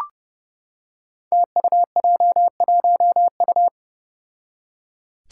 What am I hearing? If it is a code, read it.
Morse code, heard as a keyed tone, about 30 wpm, 700 Hz. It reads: TUJ1U
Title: Five Challenging Letter-Number Combinations 30wpm